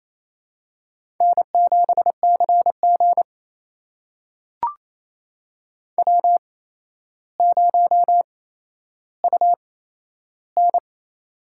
Morse code, heard as a keyed tone, about 28 wpm, spaced as for 12 wpm, 700 Hz. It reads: N7CG W0UN
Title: USA Call Signs - 4 Characters - 8x Character Spacing - 28wpm